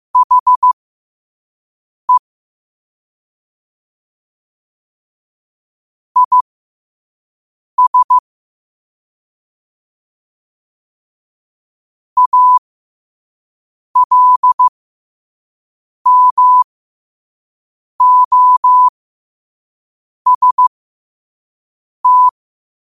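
Morse code, heard as a keyed tone, about 15 wpm, spaced as for 5 wpm, 1000 Hz. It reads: HE IS ALMOST
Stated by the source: Morse at 5 wpm.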